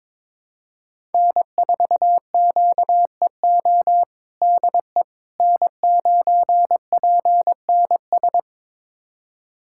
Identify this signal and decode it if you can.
Morse code, heard as a keyed tone, about 22 wpm, 700 Hz. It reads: N4QEO DE N9PNS
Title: USA Callsign DE Callsign - 5 Characters - 22wpm